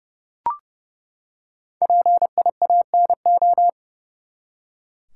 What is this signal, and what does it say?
Morse code, heard as a keyed tone, about 30 wpm, 700 Hz. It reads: PIANO
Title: Five Letter Words - 30wpm